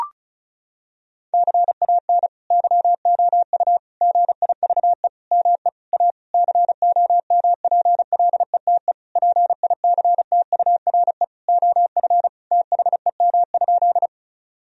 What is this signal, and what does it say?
Morse code, heard as a keyed tone, about 36 wpm, 700 Hz. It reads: CAN YOU GIVE ME A COMPLETE PICTURE OF THEM?
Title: Sentences from Top 1000 Words - Review Part 2 - 35wpm